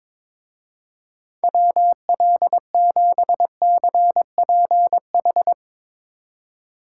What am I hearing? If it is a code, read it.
Morse code, heard as a keyed tone, about 22 wpm, 700 Hz. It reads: WL7CPH
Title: Speed Racing - USA 2x3 Call Signs 22wpm